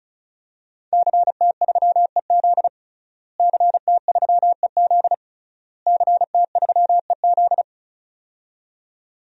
Morse code, heard as a keyed tone, about 36 wpm, 700 Hz. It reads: CT3EZ CT3EZ CT3EZ
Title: DXCC-100 Call Signs - 5 Characters - Repeated 3x - 35wpm